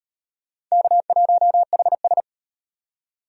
Morse code, heard as a keyed tone, about 33 wpm, 700 Hz. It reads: K1HS